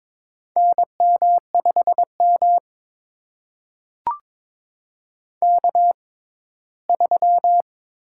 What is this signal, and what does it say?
Morse code, heard as a keyed tone, about 22 wpm, spaced as for 15 wpm, 700 Hz. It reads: NM5M K3
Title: USA Call Signs - 6x Character Spacing - 22wpm